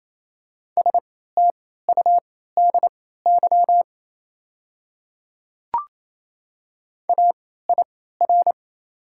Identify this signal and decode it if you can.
Morse code, heard as a keyed tone, about 28 wpm, spaced as for 16 wpm, 700 Hz. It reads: STUDY AIR
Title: 3x Character Spacing - Top 300 Words - 28wpm